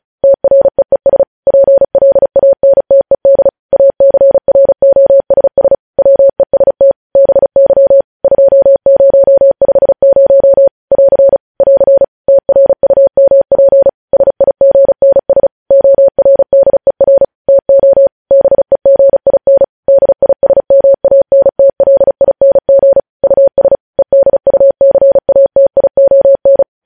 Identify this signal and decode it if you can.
Morse code, heard as a keyed tone, about 36 wpm, 550 Hz. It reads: TREES PLANTED ACROSS WEST BY 2050 <AR> <AR> TRUMP SIGNS ORDER TO BEGIN DISMANTLING US EDUCATION
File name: podcast / News headlines-1x-35x35